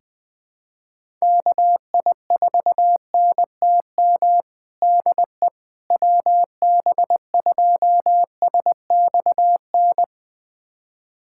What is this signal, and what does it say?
Morse code, heard as a keyed tone, about 20 wpm, 700 Hz. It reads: KI4NTM DE WB2SXN
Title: USA Callsign DE Callsign 20wpm